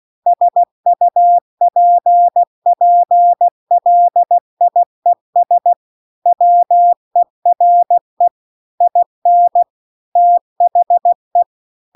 Morse code, heard as a keyed tone, about 16 wpm, 700 Hz. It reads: SUPPLIES WERE IN THE